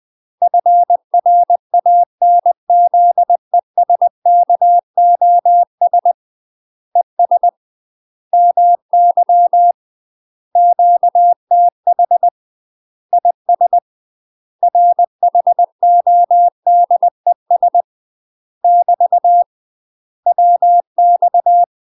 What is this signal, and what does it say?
Morse code, heard as a keyed tone, about 20 wpm, 700 Hz. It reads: FRANZESKOS ES MY QTH IS RHODES <BT> WX
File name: UK MASTER QSOs TRAINING / 20WPM / UK MASTER QSOs 0059